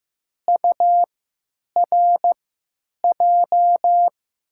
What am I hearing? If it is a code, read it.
Morse code, heard as a keyed tone, about 15 wpm, spaced as for 9 wpm, 700 Hz. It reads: URJ
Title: USA Call Signs - 3x Character Spacing - 15wpm